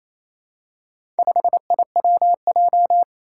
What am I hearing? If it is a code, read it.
Morse code, heard as a keyed tone, about 27 wpm, 700 Hz. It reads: 5IWJ
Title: Four Challenging Letter-Number Combinations 28wpm